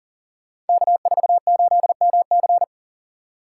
Morse code, heard as a keyed tone, about 38 wpm, 700 Hz. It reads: K48MC
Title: Code Groups - 5 Letter:Numbers - 40wpm